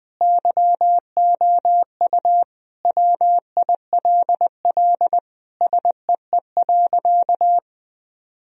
Morse code, heard as a keyed tone, about 20 wpm, 700 Hz. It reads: YOU WILL SEE.
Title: Sentences from Top 1000 Words - Review Part 1 - 20wpm